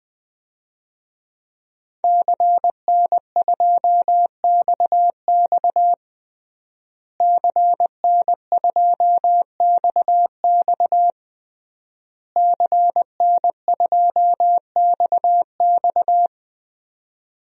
Morse code, heard as a keyed tone, about 20 wpm, 700 Hz. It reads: CN2XX CN2XX CN2XX
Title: DXCC-100 Call Signs - 5 Characters - Repeated 3x - 20wpm